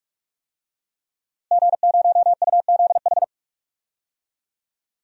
Morse code, heard as a keyed tone, about 41 wpm, 700 Hz. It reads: G0UZH
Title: DXCC-100 Call Signs - 5 Characters - Repeated 2x - 45wpm